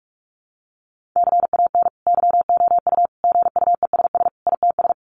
Morse code, heard as a keyed tone, about 44 wpm, 700 Hz. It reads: CAN YOU GUESS ITS
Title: Sentences from Top 1500 Words 45wpm